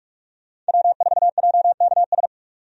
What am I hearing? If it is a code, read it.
Morse code, heard as a keyed tone, about 47 wpm, 700 Hz. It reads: W4JKS
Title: USA Call Signs - 5 Characters - 45wpm